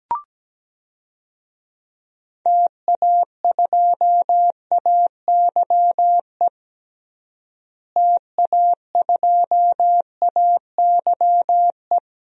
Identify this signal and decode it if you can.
Morse code, heard as a keyed tone, about 17 wpm, 700 Hz. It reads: TA2AYE TA2AYE
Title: DXCC-100 Call Signs - 6 Characters - Repeated 2x - 17wpm